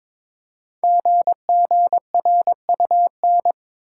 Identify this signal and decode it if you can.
Morse code, heard as a keyed tone, about 22 wpm, 700 Hz. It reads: GGRUN
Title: Code Groups - 5 Letters - 22wpm